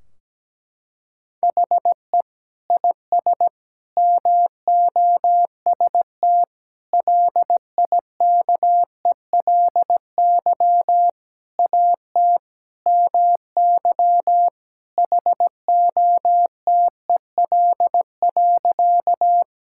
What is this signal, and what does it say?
Morse code, heard as a keyed tone, about 17 wpm, 700 Hz. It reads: HE IS MOST LIKELY AT MY HOTEL.
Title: Sentences from Top 1600 Words 17wpm